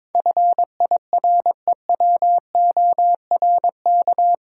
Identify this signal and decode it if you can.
Morse code, heard as a keyed tone, about 22 wpm, 700 Hz. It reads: FIREWORK
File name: practice / 22wpm / Eight Letter Words - 22wpm